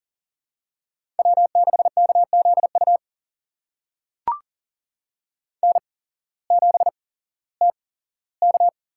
Morse code, heard as a keyed tone, about 40 wpm, spaced as for 19 wpm, 700 Hz. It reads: W6KZU N7TK
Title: USA Call Signs - 5 Characters - 8x Character Spacing - 40wpm